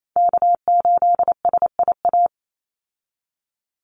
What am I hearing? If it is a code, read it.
Morse code, heard as a keyed tone, about 28 wpm, 700 Hz. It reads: K8SIA
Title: Call Signs Repeated 2x - 28wpm